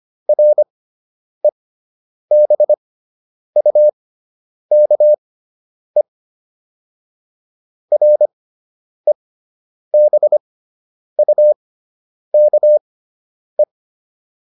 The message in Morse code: REBUKE REBUKE

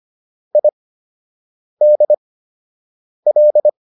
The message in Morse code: IDL

